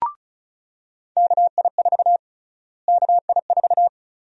Morse code, heard as a keyed tone, about 36 wpm, 700 Hz. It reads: KI4 KI4
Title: USA Call Sign Prefixes - Repeated 2x - 35wpm